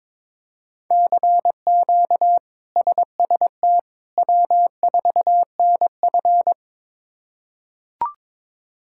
Morse code, heard as a keyed tone, about 22 wpm, 700 Hz. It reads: CQ SST W4NF
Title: K1USN SST QSO Practice - 22wpm